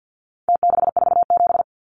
Morse code, heard as a keyed tone, about 49 wpm, 700 Hz. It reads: T647
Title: Numbers - Contest Style - 4-Digit 50wpm